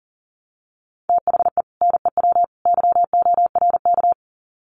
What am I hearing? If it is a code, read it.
Morse code, heard as a keyed tone, about 40 wpm, 700 Hz. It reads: THE NEW YORK
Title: Sets of 3 Words 40wpm